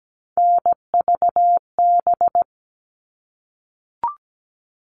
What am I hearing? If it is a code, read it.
Morse code, heard as a keyed tone, about 17 wpm, 700 Hz. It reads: NVB
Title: All Cut Numbers - 3-Digit 17wpm